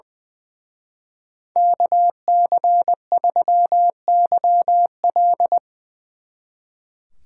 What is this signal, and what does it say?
Morse code, heard as a keyed tone, about 22 wpm, 700 Hz. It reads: KC3YL
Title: Speed Racing - USA Call Signs 20wpm